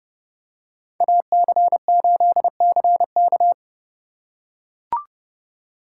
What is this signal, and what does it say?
Morse code, heard as a keyed tone, about 29 wpm, 700 Hz. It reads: AC8CK